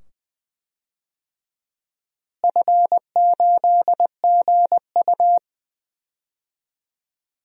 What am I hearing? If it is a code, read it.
Morse code, heard as a keyed tone, about 20 wpm, 700 Hz. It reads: F8GU
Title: DXCC-100 Call Signs - 4 Characters - Repeated 2x - 20wpm